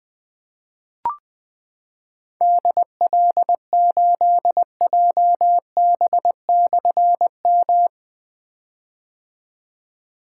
Morse code, heard as a keyed tone, about 20 wpm, 700 Hz. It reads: DL8JB/M
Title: Difficult Call Signs 20wpm